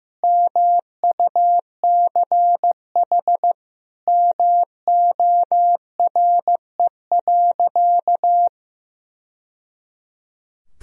Morse code, heard as a keyed tone, about 16 wpm, 700 Hz. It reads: MUCH MORE.
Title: Sentences from Top 1000 Words - Review Part 4 - 15wpm